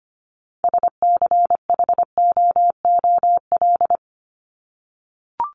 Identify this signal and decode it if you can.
Morse code, heard as a keyed tone, about 25 wpm, 700 Hz. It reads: SCHOOL
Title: Top 300 Words 25wpm